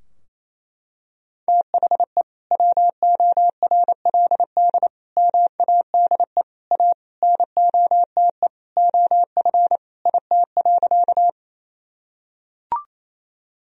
Morse code, heard as a keyed tone, about 28 wpm, 690 Hz. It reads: THE WORLD MADE A NOTE OF IT.
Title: Sentences from Top 1000 Words - Review Part 2 - 28wpm